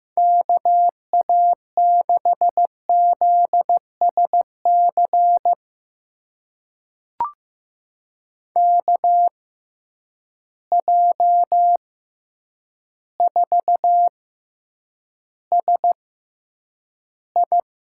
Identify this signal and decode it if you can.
Morse code, heard as a keyed tone, about 15 wpm, spaced as for 8 wpm, 700 Hz. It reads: KA6ZSC KJ4SI
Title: USA Call Signs - 6 Characters - 6x Character Spacing - 15wpm